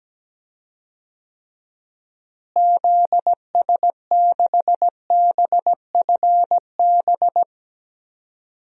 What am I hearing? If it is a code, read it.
Morse code, heard as a keyed tone, about 17 wpm, 700 Hz. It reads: ZS6BFB
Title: DXCC-100 Call Signs - 6 Characters - Repeated 2x - 17wpm